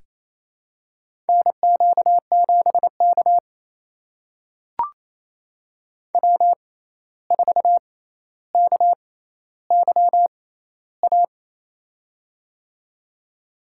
Morse code, heard as a keyed tone, about 28 wpm, spaced as for 13 wpm, 700 Hz. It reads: NQ7K W4KYA
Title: USA Call Signs - 6x Character Spacing - 28wpm